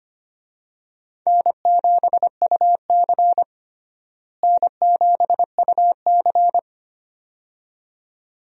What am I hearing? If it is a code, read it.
Morse code, heard as a keyed tone, about 25 wpm, 700 Hz. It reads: N7UC N7UC
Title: USA Call Signs - 4 Characters - Repeated 2x - 25wpm